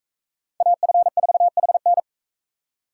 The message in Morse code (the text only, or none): AF4HN